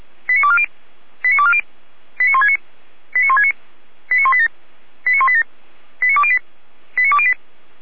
Five-tone selective call (ZVEI), heard as quick runs of five tones, tones of about 70 ms. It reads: 89240 89240 89169 89169 89187 89187 89208 89208